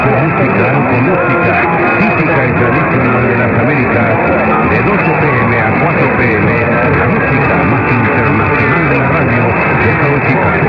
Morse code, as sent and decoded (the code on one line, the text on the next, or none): none